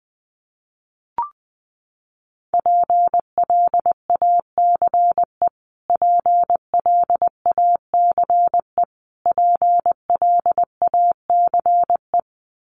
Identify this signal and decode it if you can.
Morse code, heard as a keyed tone, about 20 wpm, 700 Hz. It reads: PLACE PLACE PLACE
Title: Top 200 Words Repeated 20wpm